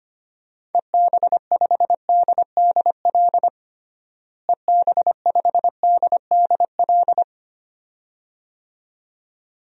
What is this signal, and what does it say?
Morse code, heard as a keyed tone, about 25 wpm, 700 Hz. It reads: EB5DDL EB5DDL
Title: DXCC-100 Call Signs - 6 Characters - Repeated 2x - 25wpm